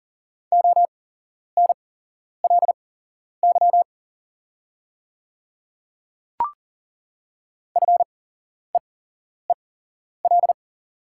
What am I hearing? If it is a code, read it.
Morse code, heard as a keyed tone, about 40 wpm, spaced as for 11 wpm, 700 Hz. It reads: ONLY FEEL
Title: 8x Character Spacing - Top 500 Words - 40wpm